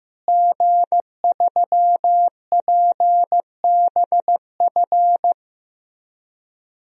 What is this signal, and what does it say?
Morse code, heard as a keyed tone, about 15 wpm, 700 Hz. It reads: G3PBF
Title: DXCC-100 Call Signs - 5 Characters - Repeated 3x - 15wpm